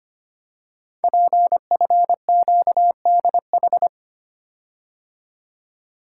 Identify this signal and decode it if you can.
Morse code, heard as a keyed tone, about 25 wpm, 700 Hz. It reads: PFQDH
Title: Code Groups - 5 Letters - 25wpm